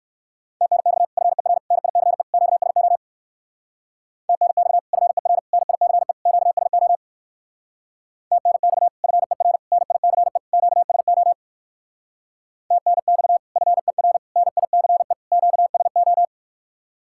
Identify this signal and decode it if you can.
Morse code, heard as a keyed tone, about 54 wpm, 700 Hz. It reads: TNX FER NICE QSO TNX FER NICE QSO TNX FER NICE QSO TNX FER NICE QSO
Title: Speed-Racing Common QSO Phrases - 45wpm